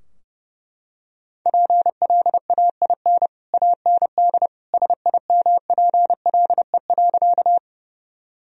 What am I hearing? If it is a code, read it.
Morse code, heard as a keyed tone, about 32 wpm, 700 Hz. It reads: PLAIN AND SIMPLE.
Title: Sentences from Top 1000 Words - Review Part 2 - 30wpm